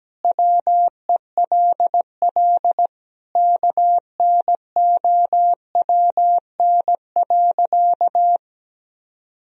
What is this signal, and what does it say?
Morse code, heard as a keyed tone, about 17 wpm, 700 Hz. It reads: WELL KNOWN.
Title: Sentences from Top 1000 Words - Review Part 4 - 17wpm